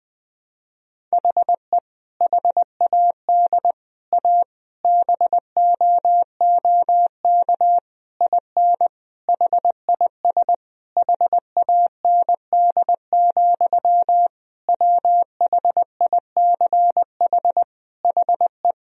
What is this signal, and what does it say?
Morse code, heard as a keyed tone, about 20 wpm, 700 Hz. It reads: HE HAD A BOOK IN HIS HAND, WHICH HE
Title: Sentences from Top 1000 Words - Review Part 3 - 20wpm